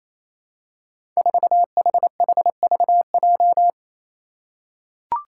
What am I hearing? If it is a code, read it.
Morse code, heard as a keyed tone, about 27 wpm, 700 Hz. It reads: 4HHVJ
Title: Five Challenging Letter-Number Combinations 28wpm